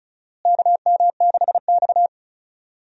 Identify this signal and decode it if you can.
Morse code, heard as a keyed tone, about 36 wpm, 700 Hz. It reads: KM6X
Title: USA Call Signs - 4 Characters - Repeated 3x - 35wpm